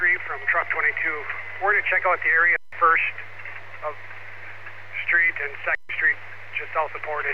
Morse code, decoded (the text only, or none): none